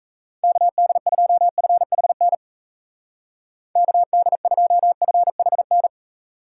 Morse code, heard as a keyed tone, about 36 wpm, 700 Hz. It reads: KD2FHN KD2FHN